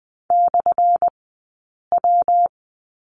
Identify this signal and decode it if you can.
Morse code, heard as a keyed tone, about 20 wpm, spaced as for 8 wpm, 700 Hz. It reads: /W
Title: 177 Lesson 29d - Slash - Two Characters 20wpm